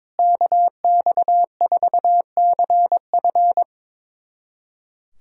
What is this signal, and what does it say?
Morse code, heard as a keyed tone, about 22 wpm, 700 Hz. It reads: KX4CF
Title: USA Callsign DE Callsign 22wpm